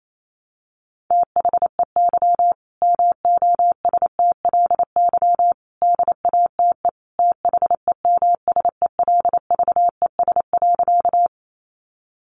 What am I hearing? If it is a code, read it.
Morse code, heard as a keyed tone, about 28 wpm, 700 Hz. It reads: THEY MOSTLY DATE THEMSELVES.
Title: Sentences from Top 1300 Words 28wpm